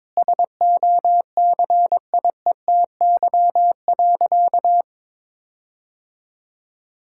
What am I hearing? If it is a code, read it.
Morse code, heard as a keyed tone, about 22 wpm, 700 Hz. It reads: SOCIETY.
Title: Sentences from Top 1600 Words 22wpm